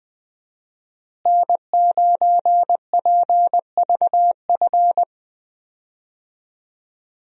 Morse code, heard as a keyed tone, about 20 wpm, 700 Hz. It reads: N9PVF
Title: USA Call Signs - 5 Characters - Repeated 2x - 20wpm